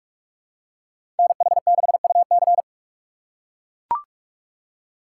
Morse code, heard as a keyed tone, about 41 wpm, 700 Hz. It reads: NH6UC
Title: USA Call Signs - 6x Character Spacing - 45wpm